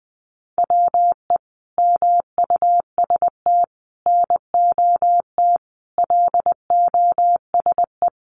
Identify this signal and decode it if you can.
Morse code, heard as a keyed tone, about 20 wpm, 700 Hz. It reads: WE MUST NOT LOSE
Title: Sentences from Top 1100 Words 20wpm